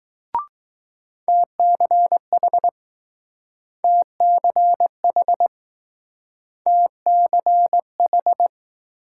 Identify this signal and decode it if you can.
Morse code, heard as a keyed tone, about 20 wpm, 700 Hz. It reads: TCH TCH TCH